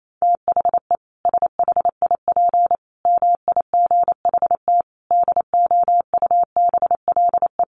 Morse code, heard as a keyed tone, about 28 wpm, 700 Hz. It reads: THE SHIP MIGHT DOUBLE
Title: Sentences from Top 1400 Words 28wpm